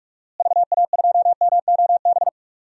Morse code, heard as a keyed tone, about 43 wpm, 700 Hz. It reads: VA2MOB
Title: DXCC-100 Call Signs - 6 Characters - Repeated 3x - 45wpm